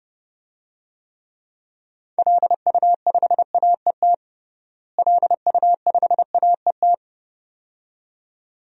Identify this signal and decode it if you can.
Morse code, heard as a keyed tone, about 29 wpm, 700 Hz. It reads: LU5AET LU5AET
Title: DXCC-100 Call Signs - 6 Characters - Repeated 2x - 30wpm